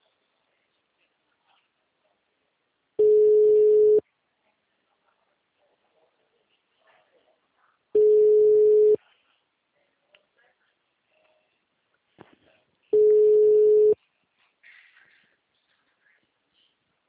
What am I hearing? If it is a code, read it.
Ringback tone (Europe, 425 Hz), with speech.